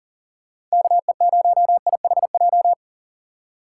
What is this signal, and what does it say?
Morse code, heard as a keyed tone, about 40 wpm, 700 Hz. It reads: KE0IHJ
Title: USA Call Signs - 6 Characters - 8x Character Spacing - 40wpm